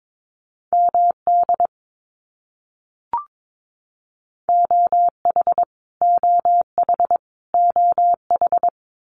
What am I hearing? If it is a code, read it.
Morse code, heard as a keyed tone, about 22 wpm, 700 Hz. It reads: MD OH OH OH
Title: US States Repeated 3x 22wpm